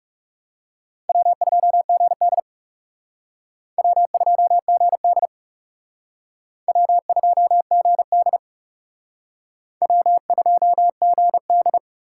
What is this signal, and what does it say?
Morse code, heard as a keyed tone, about 37 wpm, 700 Hz. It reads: W2GD W2GD W2GD W2GD
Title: Speed Racing - USA Call Signs 30wpm